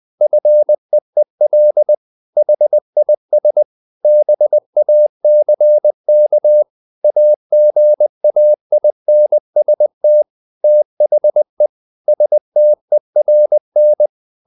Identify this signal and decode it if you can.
Morse code, heard as a keyed tone, about 20 wpm, 600 Hz. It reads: FEEL HIS BACK AGAINST THE STERN
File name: the old man and the sea part 2 20wpm